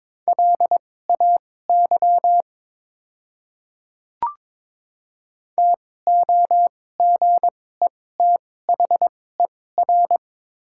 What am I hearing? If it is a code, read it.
Morse code, heard as a keyed tone, about 22 wpm, spaced as for 16 wpm, 700 Hz. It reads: LAY TOGETHER
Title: 2x Character Spacing - Top 500 Words - 22wpm